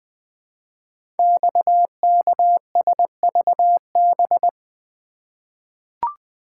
Morse code, heard as a keyed tone, about 20 wpm, 700 Hz. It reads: XKSVB
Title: Code Groups - 5 Letters - 20wpm